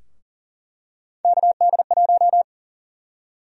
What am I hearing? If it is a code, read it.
Morse code, heard as a keyed tone, about 38 wpm, 700 Hz. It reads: KD1
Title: USA Call Sign Prefixes - 40wpm